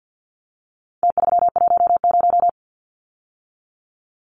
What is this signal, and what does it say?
Morse code, heard as a keyed tone, about 50 wpm, 700 Hz. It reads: T310
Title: Numbers - Contest Style - 4-Digit 50wpm